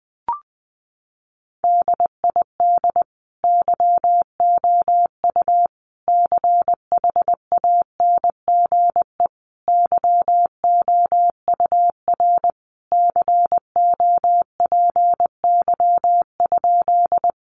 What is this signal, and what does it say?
Morse code, heard as a keyed tone, about 19 wpm, 700 Hz. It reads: DID YOU CHANGE YOUR COPY?
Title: Sentences from Top 1400 Words 20wpm